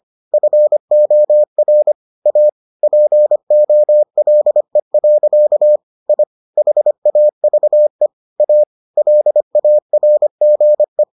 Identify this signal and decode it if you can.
Morse code, heard as a keyed tone, about 25 wpm, 600 Hz. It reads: FOR A POLE. I HAVE A LARGE